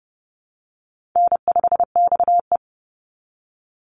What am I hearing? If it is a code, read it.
Morse code, heard as a keyed tone, about 30 wpm, 700 Hz. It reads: N5XE
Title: Easy Call Signs Repeated 2x - 30wpm